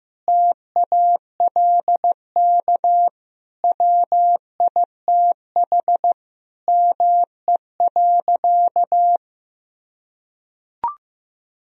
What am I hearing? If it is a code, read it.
Morse code, heard as a keyed tone, about 15 wpm, 700 Hz. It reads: TALK WITH ME.
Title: Sentences from Top 1000 Words - Review Part 3 - 15wpm